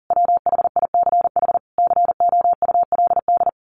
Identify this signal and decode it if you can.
Morse code, heard as a keyed tone, about 40 wpm, 700 Hz. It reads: WHICH COULD
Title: A Sound Of Thunder 40wpm